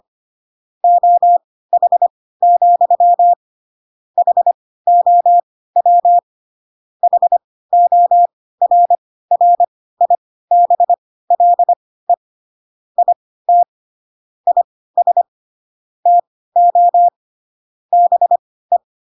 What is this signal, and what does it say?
Morse code, heard as a keyed tone, about 25 wpm, spaced as for 16 wpm, 700 Hz. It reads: OH, HOW HORRIBLE IT IS TO BE
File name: Pinocchio / Pinocchio ch5 10 25 ik1hge